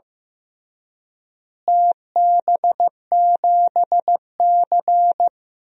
Morse code, heard as a keyed tone, about 15 wpm, 700 Hz. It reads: TB7C
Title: DXCC-100 Call Signs - 4 Characters - Repeated 3x - 15wpm